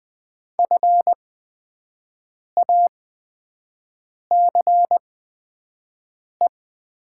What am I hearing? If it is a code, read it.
Morse code, heard as a keyed tone, about 20 wpm, spaced as for 5 wpm, 700 Hz. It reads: FACE
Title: 8x Character Spacing - Top 500 Words - 20wpm